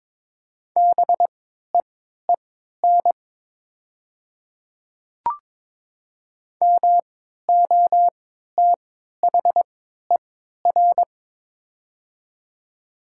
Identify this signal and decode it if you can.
Morse code, heard as a keyed tone, about 22 wpm, spaced as for 13 wpm, 700 Hz. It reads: BEEN MOTHER
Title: 3x Character Spacing - Top 200 Words - 22wpm